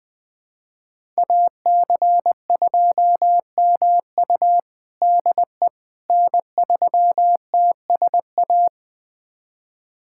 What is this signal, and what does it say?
Morse code, heard as a keyed tone, about 21 wpm, 700 Hz. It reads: AC2MU DE N3TSA
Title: USA Callsign DE Callsign - 5 Characters - 20wpm